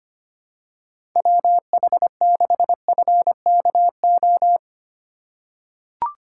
Morse code, heard as a keyed tone, about 25 wpm, 700 Hz. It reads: WH6FKO
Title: USA Call Signs - 6 Characters - 4x Character Spacing - 25wpm